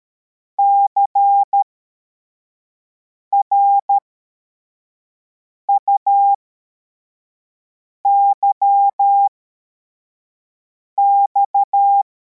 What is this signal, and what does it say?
Morse code, heard as a keyed tone, about 13 wpm, spaced as for 4 wpm, 800 Hz. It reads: CRUYX